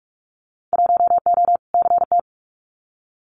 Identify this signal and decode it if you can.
Morse code, heard as a keyed tone, about 44 wpm, 700 Hz. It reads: 1O CT
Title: Winter Field Day Exchanges Encore 45wpm